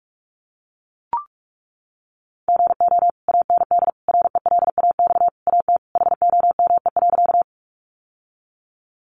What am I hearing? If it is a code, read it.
Morse code, heard as a keyed tone, about 44 wpm, 700 Hz. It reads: GO AND RELAX AT HOME.
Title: Sentences from Top 1400 Words 45wpm